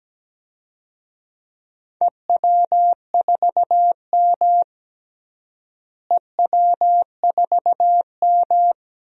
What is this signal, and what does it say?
Morse code, heard as a keyed tone, about 17 wpm, 700 Hz. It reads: EW4M EW4M
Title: DXCC-100 Call Signs - 4 Characters - Repeated 3x - 17wpm